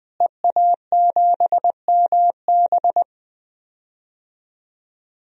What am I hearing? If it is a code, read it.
Morse code, heard as a keyed tone, about 20 wpm, 700 Hz. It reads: EA7MB
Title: DXCC-100 Call Signs - 5 Characters - Repeated 2x - 20wpm